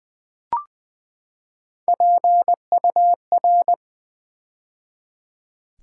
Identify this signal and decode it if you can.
Morse code, heard as a keyed tone, about 20 wpm, 700 Hz. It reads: PUR
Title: Build A Word - 3 Letters At A Time - 6-Letter Words - 20wpm